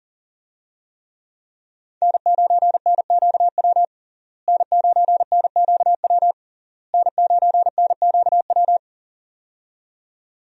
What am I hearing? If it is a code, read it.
Morse code, heard as a keyed tone, about 37 wpm, 700 Hz. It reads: N9NQW N9NQW N9NQW
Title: USA Call Signs - 5 Characters - Repeated 3x - 40wpm